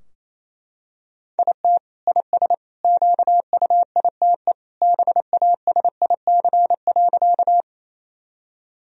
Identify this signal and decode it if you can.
Morse code, heard as a keyed tone, about 28 wpm, 700 Hz. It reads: IT IS QUITE BASIC.
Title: Sentences from Top 1000 Words - Review Part 3 - 28wpm